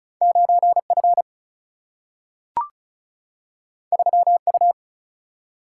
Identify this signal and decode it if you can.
Morse code, heard as a keyed tone, about 36 wpm, 700 Hz. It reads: 9F 3U
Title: Single-Letter Single-Number 35wpm